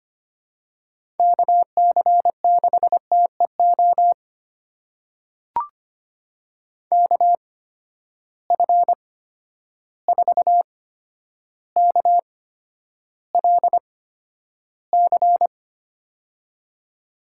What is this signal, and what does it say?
Morse code, heard as a keyed tone, about 25 wpm, spaced as for 11 wpm, 700 Hz. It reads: KC6TEO KF4KLC
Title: USA Call Signs - 6 Characters - 8x Character Spacing - 25wpm